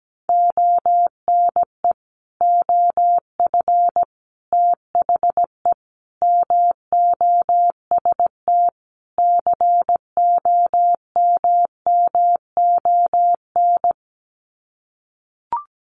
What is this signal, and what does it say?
Morse code, heard as a keyed tone, about 17 wpm, 700 Hz. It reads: ONE OF THE MOST COMMON
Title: Sets of 5 Words Encore2 17wpm